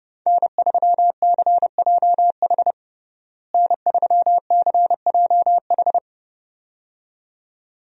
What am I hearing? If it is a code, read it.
Morse code, heard as a keyed tone, about 30 wpm, 700 Hz. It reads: N3CJH N3CJH
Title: USA Call Signs - 5 Characters - Repeated 2x - 30wpm